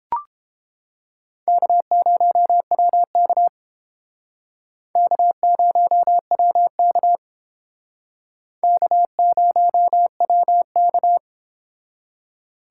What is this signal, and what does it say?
Morse code, heard as a keyed tone, about 29 wpm, 700 Hz. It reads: K0WK K0WK K0WK